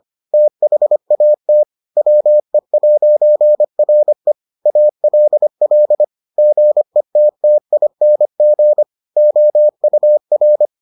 Morse code, heard as a keyed tone, about 25 wpm, 600 Hz. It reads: THAT WE'RE ALL GETTING OUR